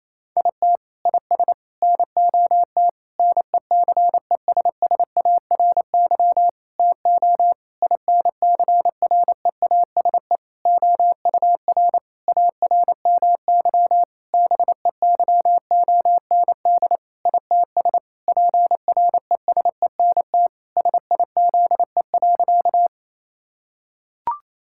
Morse code, heard as a keyed tone, about 28 wpm, 700 Hz. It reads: IT IS NOT NECESSARY TO INCREASE OUR ARMY BEYOND ITS PRESENT SIZE.
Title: Sentences from Top 1000 Words - Review Part 6 - 28wpm